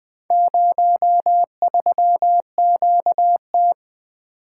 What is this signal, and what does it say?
Morse code, heard as a keyed tone, about 20 wpm, 700 Hz. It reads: 03QT
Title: Four Letter-Number Combinations - 20wpm